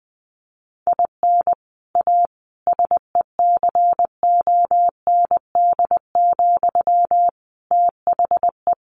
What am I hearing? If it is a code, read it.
Morse code, heard as a keyed tone, about 20 wpm, 700 Hz. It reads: IN A SECOND, THE
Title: Sentences from Top 1500 Words 20wpm